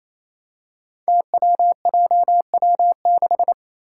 Morse code, heard as a keyed tone, about 28 wpm, 700 Hz. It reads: TWJW6
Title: Five Challenging Letter-Number Combinations 28wpm